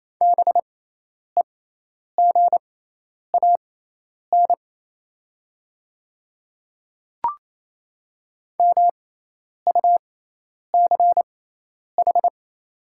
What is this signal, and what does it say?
Morse code, heard as a keyed tone, about 28 wpm, spaced as for 10 wpm, 690 Hz. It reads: BEGAN MUCH
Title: 6x Character Spacing - Top 500 Words - 28wpm